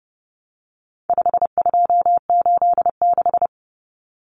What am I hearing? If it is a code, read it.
Morse code, heard as a keyed tone, about 30 wpm, 700 Hz. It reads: 5286
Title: Numbers - 4-Digit 30wpm